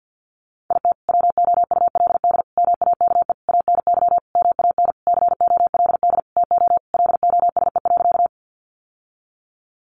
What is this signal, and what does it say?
Morse code, heard as a keyed tone, about 50 wpm, 700 Hz. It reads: IT WOULD MAKE ANY MAN COLD TO LOSE.
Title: Sentences from Top 1100 Words 50wpm